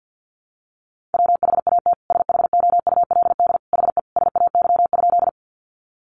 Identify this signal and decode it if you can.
Morse code, heard as a keyed tone, about 50 wpm, 700 Hz. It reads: WHAT SHOULD HE SAY?
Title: Sentences from Top 300 Words 50wpm